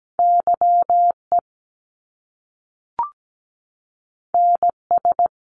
Morse code, heard as a keyed tone, about 17 wpm, 700 Hz. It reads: YE NS
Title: Top Two Letter Combinations in Words 17wpm